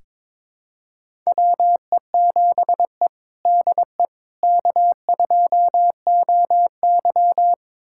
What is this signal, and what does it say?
Morse code, heard as a keyed tone, about 22 wpm, 700 Hz. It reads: WE7E DE K2OY
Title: USA Callsign DE Callsign - 4 Characters - 22wpm